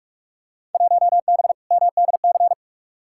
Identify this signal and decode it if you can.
Morse code, heard as a keyed tone, about 42 wpm, 700 Hz. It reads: 1B MDC
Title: Field Day Exchanges Encore - 45wpm